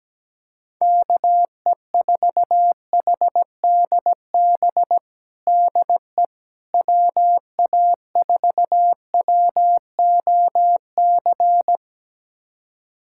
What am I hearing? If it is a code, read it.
Morse code, heard as a keyed tone, about 17 wpm, 700 Hz. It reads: KE4HDB DE WA4WOC
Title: USA Callsign DE Callsign - 6 Characters - 17wpm